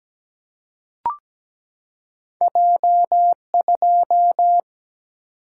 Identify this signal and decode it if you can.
Morse code, heard as a keyed tone, about 17 wpm, 700 Hz. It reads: J2
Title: Single-Letter Single-Number 17wpm